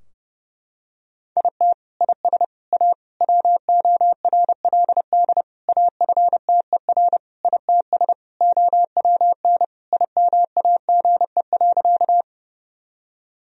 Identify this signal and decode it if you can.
Morse code, heard as a keyed tone, about 30 wpm, 700 Hz. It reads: IT IS A WORLD AFTER ITS OWN IMAGE.
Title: Sentences from Top 1000 Words - Review Part 4 - 30wpm